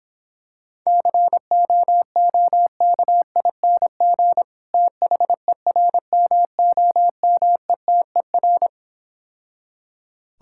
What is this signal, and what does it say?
Morse code, heard as a keyed tone, about 26 wpm, 700 Hz. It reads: COOKING THERMOMETER